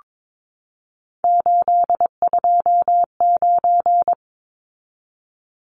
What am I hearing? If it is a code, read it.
Morse code, heard as a keyed tone, about 22 wpm, 700 Hz. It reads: 829
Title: Numbers - 3-Digit 22wpm